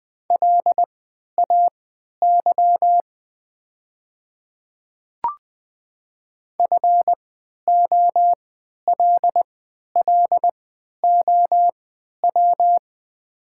Morse code, heard as a keyed tone, about 20 wpm, spaced as for 11 wpm, 700 Hz. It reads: LAY FOLLOW
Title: 3x Character Spacing - Top 500 Words - 20wpm